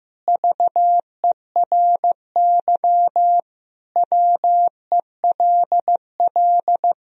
Morse code, heard as a keyed tone, about 15 wpm, 700 Hz. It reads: VERY WELL
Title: Sentences from Top 1000 Words - Review Part 3 - 15wpm